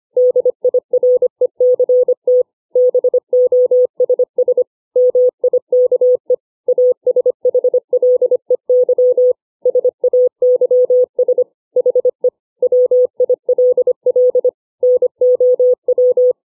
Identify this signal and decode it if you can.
Morse code, heard as a keyed tone, about 25 wpm, 500 Hz. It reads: DIRECT BOSS MIKE ASHLEY SAYS HE WILL NOW